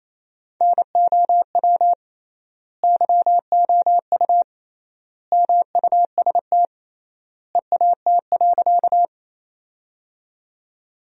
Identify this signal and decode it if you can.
Morse code, heard as a keyed tone, about 28 wpm, 700 Hz. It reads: NOW YOU MUST EAT.
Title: 3x Word-Spacing - Sentences from Top 300 Words - 28wpm